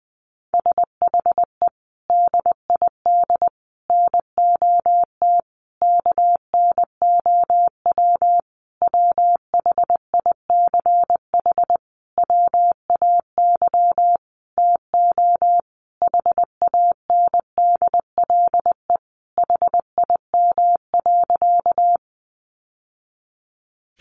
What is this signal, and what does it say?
Morse code, heard as a keyed tone, about 20 wpm, 700 Hz. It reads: SHE DID NOT KNOW WHICH WAY TO HANDLE HIM.
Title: Sentences from Top 1100 Words 20wpm